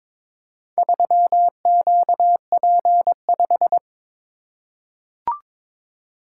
Morse code, heard as a keyed tone, about 22 wpm, 700 Hz. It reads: 3QP5
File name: practice / 22wpm / Four Letter-Number Combinations - 22wpm